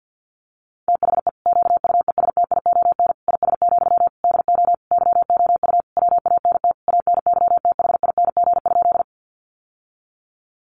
Morse code, heard as a keyed tone, about 48 wpm, 700 Hz. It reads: THE QUESTION IS, DO YOU WANT ANYTHING?
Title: Sentences from Top 1000 Words 50wpm